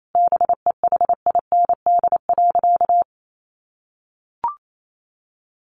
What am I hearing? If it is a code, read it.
Morse code, heard as a keyed tone, about 28 wpm, 700 Hz. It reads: BEHIND.
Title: Sentences from Top 1400 Words 28wpm